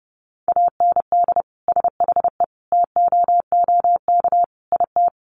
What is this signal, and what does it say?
Morse code, heard as a keyed tone, about 30 wpm, 700 Hz. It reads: AND SHE TOOK IT
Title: Sentences from Top 1400 Words 30wpm